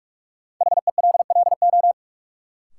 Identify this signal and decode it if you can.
Morse code, heard as a keyed tone, about 46 wpm, 700 Hz. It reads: HELLO
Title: Sentences from Top 1100 Words - New Word Drill Alt - 45wpm